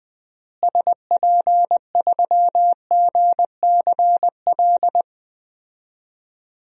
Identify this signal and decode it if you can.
Morse code, heard as a keyed tone, about 20 wpm, 700 Hz. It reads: SP3GCL
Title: DXCC-100 Call Signs - 6 Characters - Repeated 2x - 20wpm